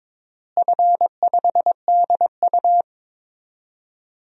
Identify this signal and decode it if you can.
Morse code, heard as a keyed tone, about 22 wpm, 700 Hz. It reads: F5DU
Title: DXCC-100 Call Signs - 4 Characters - Repeated 2x - 22wpm